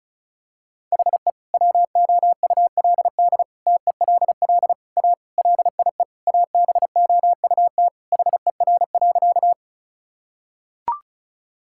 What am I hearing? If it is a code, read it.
Morse code, heard as a keyed tone, about 36 wpm, 700 Hz. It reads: HE WOULD TELL A LIE ABOUT HER.
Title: Sentences from Top 1600 Words 35wpm